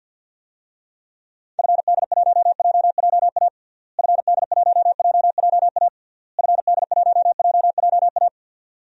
Morse code, heard as a keyed tone, about 51 wpm, 700 Hz. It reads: FD1JJA FD1JJA FD1JJA
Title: DXCC-100 Call Signs - 6 Characters - Repeated 3x - 50wpm